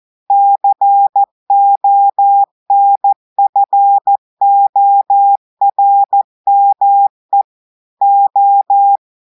Morse code, heard as a keyed tone, about 14 wpm, 800 Hz. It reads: CONFORME O
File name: audios / portugais / portugais texte en clair 2 vitesse 14 mots minutes